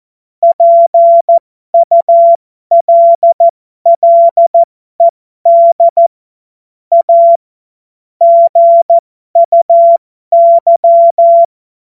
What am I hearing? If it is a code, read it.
Morse code, heard as a keyed tone, about 14 wpm, spaced as for 12 wpm, 680 Hz. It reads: PULLED A GUY
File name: Morsekurs / Witze